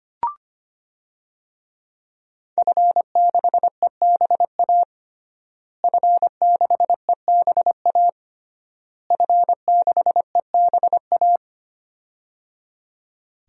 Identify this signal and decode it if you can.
Morse code, heard as a keyed tone, about 25 wpm, 700 Hz. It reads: F6EBA F6EBA F6EBA
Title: DXCC-100 Call Signs - 5 Characters - Repeated 3x - 25wpm